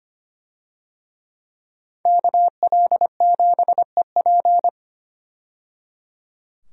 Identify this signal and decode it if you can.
Morse code, heard as a keyed tone, about 25 wpm, 700 Hz. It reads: KL7EP
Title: DXCC-100 Call Signs - 5 Characters - 25wpm